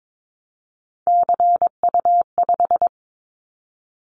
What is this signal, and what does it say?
Morse code, heard as a keyed tone, about 22 wpm, 700 Hz. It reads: CU5
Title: Callsign Prefixes for Top 50 DXCC - 22wpm